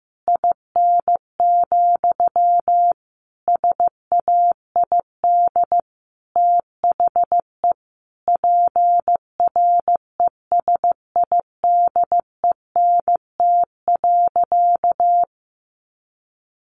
Morse code, heard as a keyed tone, about 15 wpm, 700 Hz. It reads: IN, SAID THE PRESIDENT.
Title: Sentences from Top 1000 Words 15wpm